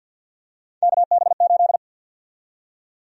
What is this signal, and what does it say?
Morse code, heard as a keyed tone, about 49 wpm, 700 Hz. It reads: KB8
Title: USA Call Sign Prefixes - 50wpm